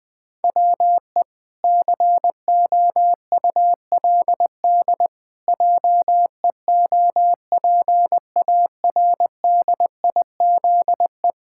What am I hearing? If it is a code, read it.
Morse code, heard as a keyed tone, about 20 wpm, 700 Hz. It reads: WE COULD JEOPARDIZE
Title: Pangram Challenge v2 - 20wpm